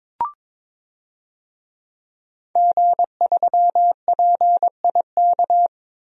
Morse code, heard as a keyed tone, about 23 wpm, 700 Hz. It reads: G3PIK
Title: DXCC-100 Call Signs - 5 Characters - 22wpm